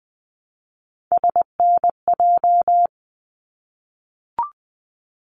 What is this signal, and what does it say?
Morse code, heard as a keyed tone, about 20 wpm, 700 Hz. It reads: SNJ
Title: ARRL Sections 20wpm